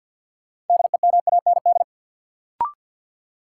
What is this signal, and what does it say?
Morse code, heard as a keyed tone, about 53 wpm, 700 Hz. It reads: DEMAND
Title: Sentences from Top 800 Words - New Word Drill - 50wpm